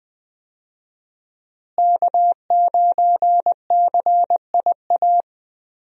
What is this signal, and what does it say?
Morse code, heard as a keyed tone, about 20 wpm, 700 Hz. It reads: K9CIA
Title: USA Call Signs - 5 Characters - Repeated 3x - 20wpm